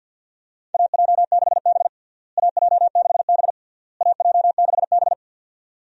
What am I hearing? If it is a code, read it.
Morse code, heard as a keyed tone, about 52 wpm, 700 Hz. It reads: AJ6B AJ6B AJ6B
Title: USA Call Signs - 4 Characters - Repeated 3x - 50wpm